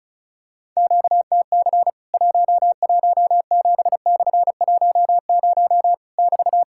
Morse code, heard as a keyed tone, about 36 wpm, 700 Hz. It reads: QTC 117/10 <BT>
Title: Worked All Europe DX Contest - QTC Exchange - 35wpm